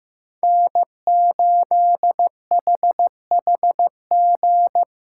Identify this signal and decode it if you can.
Morse code, heard as a keyed tone, about 15 wpm, 700 Hz. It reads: N8HHG
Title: K1USN SST QSO Practice - 15wpm